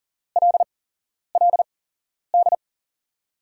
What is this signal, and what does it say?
Morse code, heard as a keyed tone, about 37 wpm, spaced as for 11 wpm, 700 Hz. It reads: LLD
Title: USA Call Signs - 8x Character Spacing - 40wpm